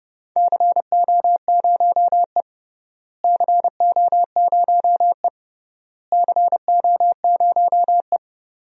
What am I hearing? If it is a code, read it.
Morse code, heard as a keyed tone, about 31 wpm, 700 Hz. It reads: CO0E CO0E CO0E
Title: DXCC-100 Call Signs - 4 Characters - Repeated 3x - 30wpm